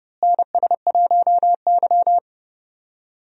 Morse code, heard as a keyed tone, about 30 wpm, 700 Hz. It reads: NS1Y
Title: USA Call Signs - 4 Characters - Repeated 3x - 30wpm